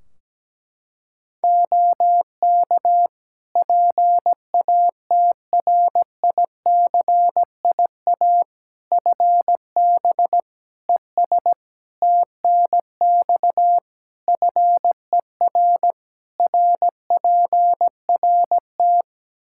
Morse code, heard as a keyed tone, about 17 wpm, 700 Hz. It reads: OK PATRICIA FB ES TNX FER RPRT
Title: Practice QSO - Exchange 2 - Info - 17wpm